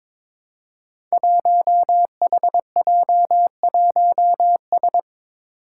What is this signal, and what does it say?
Morse code, heard as a keyed tone, about 22 wpm, 700 Hz. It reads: 1HJ1S
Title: Five Challenging Letter-Number Combinations 22wpm